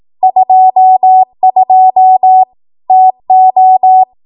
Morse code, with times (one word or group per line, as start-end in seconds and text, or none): 0.2-2.4 22
2.9-4.0 TO